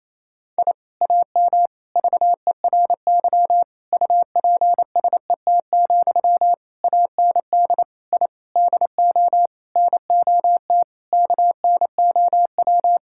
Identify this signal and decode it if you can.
Morse code, heard as a keyed tone, about 28 wpm, 700 Hz. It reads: I AM VERY UPSET, AND I DO NOT KNOW
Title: Sentences from Top 1000 Words - Review Part 5 - 28wpm